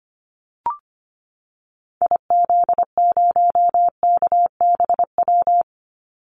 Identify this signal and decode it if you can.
Morse code, heard as a keyed tone, about 25 wpm, 700 Hz. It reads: IZ0KBW
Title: Call Signs Repeated 2x - 25wpm